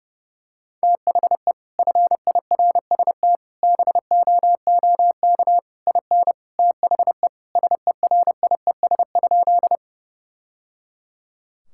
Morse code, heard as a keyed tone, about 32 wpm, 700 Hz. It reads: THE FIRST BOOK IN THE SERIES?
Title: Sentences from Top 1000 Words - Review Part 5 - 30wpm